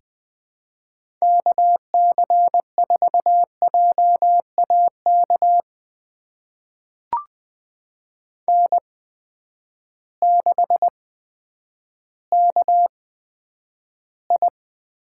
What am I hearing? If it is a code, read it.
Morse code, heard as a keyed tone, about 20 wpm, spaced as for 10 wpm, 700 Hz. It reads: KC4JAK N6KI
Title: USA Call Signs - 8x Character Spacing - 20wpm